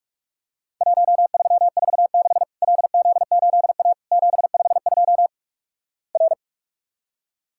Morse code, heard as a keyed tone, about 46 wpm, 700 Hz. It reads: 1346 LZ8A 752 R
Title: Worked All Europe DX Contest - QTC Exchange - 45wpm